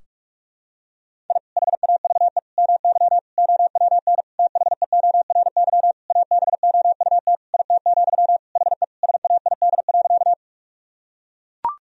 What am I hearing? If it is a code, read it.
Morse code, heard as a keyed tone, about 41 wpm, 700 Hz. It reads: I HAVE MY OWN THEORY ABOUT IT, HE SAID.
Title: Sentences from Top 1000 Words - Review Part 3 - 45wpm